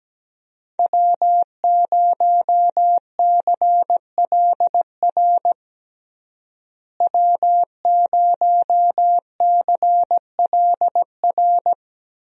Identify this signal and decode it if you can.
Morse code, heard as a keyed tone, about 17 wpm, 700 Hz. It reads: W0CLR W0CLR
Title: USA Call Signs - 5 Characters - Repeated 3x - 17wpm